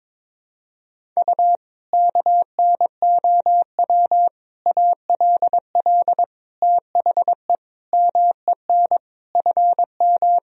Morse code, heard as a keyed tone, about 22 wpm, 700 Hz. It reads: U KNOW ALL THE MEN FM
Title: Sentences from Top 300 Words and Abbreviations 22wpm